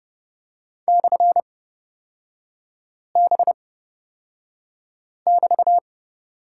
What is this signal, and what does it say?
Morse code, heard as a keyed tone, about 29 wpm, spaced as for 5 wpm, 700 Hz. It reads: /B<BT>
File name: Lesson 36e - BT - Three Characters 30wpm